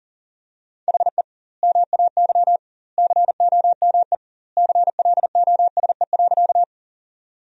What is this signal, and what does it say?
Morse code, heard as a keyed tone, about 40 wpm, 700 Hz. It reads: HE MAY COME CLOSE.
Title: 2x Word-Spacing - Sentences from Top 300 Words - 40wpm